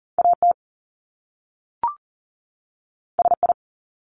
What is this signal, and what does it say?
Morse code, heard as a keyed tone, about 38 wpm, 700 Hz. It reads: AT SI
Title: Top Two Letter Combinations in Words 40wpm